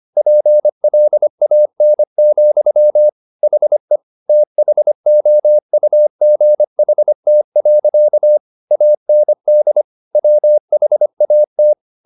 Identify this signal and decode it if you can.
Morse code, heard as a keyed tone, about 25 wpm, 600 Hz. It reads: PLAN, HE THOUGHT. AND WHAT
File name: the old man and the sea part 1 25wpm